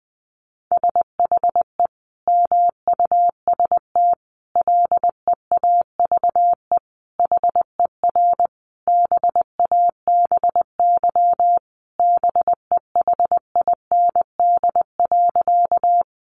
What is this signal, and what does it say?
Morse code, heard as a keyed tone, about 20 wpm, 700 Hz. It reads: SHE MUST LEAVE HER BABY BEHIND.
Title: Sentences from Top 1300 Words 20wpm